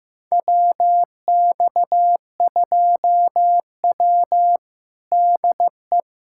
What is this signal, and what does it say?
Morse code, heard as a keyed tone, about 15 wpm, 700 Hz. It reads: WX2W DE
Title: USA Callsign DE Callsign - 4 Characters - 15wpm